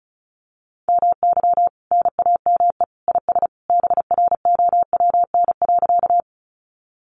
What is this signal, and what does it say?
Morse code, heard as a keyed tone, about 34 wpm, 700 Hz. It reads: MY NAME IS BROWN.
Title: Sentences from Top 1200 Words 35wpm